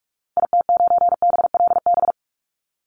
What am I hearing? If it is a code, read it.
Morse code, heard as a keyed tone, about 44 wpm, 700 Hz. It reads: IT9BLB
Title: Call Signs Repeated 3x - 45wpm